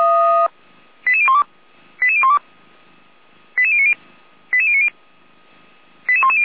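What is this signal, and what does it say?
Five-tone selective call (ZVEI), heard as quick runs of five tones, tones of about 80 ms. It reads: 88012 88012 88090 88090 88110
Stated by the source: SDR-received ZVEI five-tone selective call, code 88102, recorded by feuerwehr-orlamuende.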